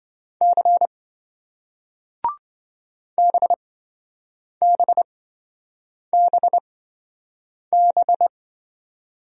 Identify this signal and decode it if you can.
Morse code, heard as a keyed tone, about 27 wpm, spaced as for 7 wpm, 700 Hz. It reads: C BBBB